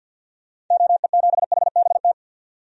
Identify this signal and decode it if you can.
Morse code, heard as a keyed tone, about 48 wpm, 700 Hz. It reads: OE7HBT
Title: DXCC-100 Call Signs - 6 Characters - 50wpm